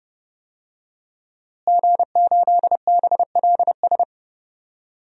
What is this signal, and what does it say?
Morse code, heard as a keyed tone, about 30 wpm, 700 Hz. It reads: G8BLS
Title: DXCC-100 Call Signs - 5 Characters - Repeated 2x - 30wpm